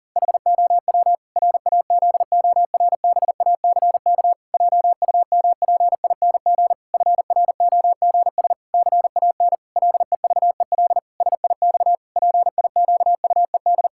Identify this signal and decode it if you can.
Morse code, heard as a keyed tone, about 37 wpm, 700 Hz. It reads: HOW RAZORBACK JUMPING FROGS CAN LEVEL SIX PIQUED
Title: Pangram Challenge v2 - 40wpm